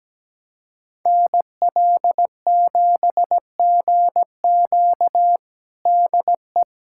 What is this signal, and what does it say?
Morse code, heard as a keyed tone, about 17 wpm, 700 Hz. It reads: NL7GQ DE
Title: USA Callsign DE Callsign - 5 Characters - 17wpm